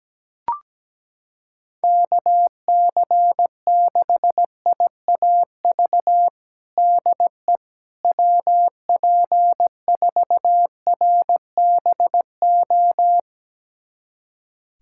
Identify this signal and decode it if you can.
Morse code, heard as a keyed tone, about 17 wpm, 700 Hz. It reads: KC6IAV DE WP4RBO
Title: USA Callsign DE Callsign 17wpm